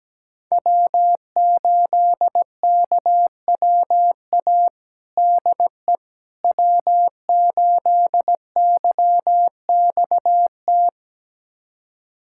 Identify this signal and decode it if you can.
Morse code, heard as a keyed tone, about 17 wpm, 700 Hz. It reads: W8KWA DE W8YXT
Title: USA Callsign DE Callsign - 5 Characters - 17wpm